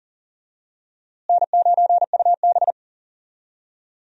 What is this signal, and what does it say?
Morse code, heard as a keyed tone, about 40 wpm, 700 Hz. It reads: N9UB
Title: USA Call Signs - 4 Characters - Repeated 3x - 40wpm